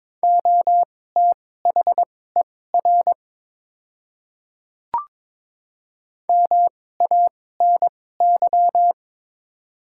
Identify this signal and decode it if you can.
Morse code, heard as a keyed tone, about 22 wpm, spaced as for 16 wpm, 700 Hz. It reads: OTHER MANY
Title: 2x Character Spacing - Top 200 Words - 22wpm